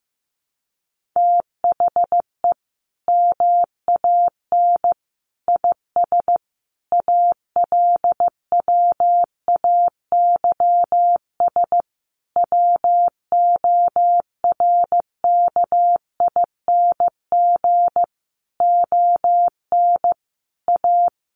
Morse code, heard as a keyed tone, about 15 wpm, 700 Hz. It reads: THE MAN IS ALWAYS WORKING ON A
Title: Sentences from Top 1400 Words 15wpm